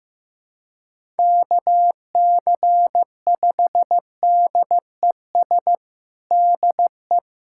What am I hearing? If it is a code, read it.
Morse code, heard as a keyed tone, about 15 wpm, 700 Hz. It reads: KC5DES DE
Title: USA Callsign DE Callsign - 6 Characters - 15wpm